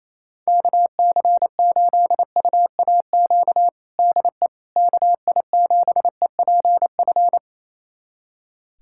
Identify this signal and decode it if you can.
Morse code, heard as a keyed tone, about 28 wpm, 700 Hz. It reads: KC8UAQ DE KI7EPF
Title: USA Callsign DE Callsign - 6 Characters - 28wpm